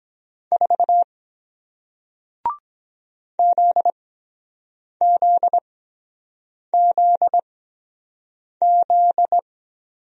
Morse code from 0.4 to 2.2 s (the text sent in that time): 4